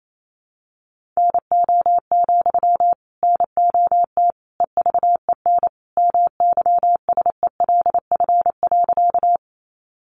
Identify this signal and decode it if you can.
Morse code, heard as a keyed tone, about 28 wpm, 700 Hz. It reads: NO, NOT EVEN MYSELF.
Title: Sentences from Top 1300 Words 28wpm